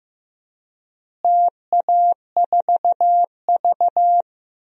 Morse code, heard as a keyed tone, about 15 wpm, 700 Hz. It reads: TA4V
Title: DXCC-100 Call Signs - 4 Characters - Repeated 3x - 15wpm